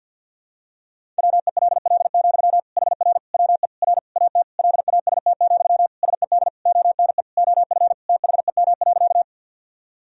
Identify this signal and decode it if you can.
Morse code, heard as a keyed tone, about 50 wpm, 700 Hz. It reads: WELL, HR WE R AT LAST, SED ONE OF THEM.
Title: Sentences from Top 300 Words and Abbreviations 50wpm